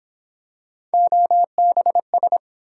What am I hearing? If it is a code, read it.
Morse code, heard as a keyed tone, about 31 wpm, 700 Hz. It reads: OBS